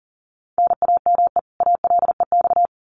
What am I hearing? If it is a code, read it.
Morse code, heard as a keyed tone, about 40 wpm, 700 Hz. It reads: NAME ALEX
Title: Common QSO Phrases 40wpm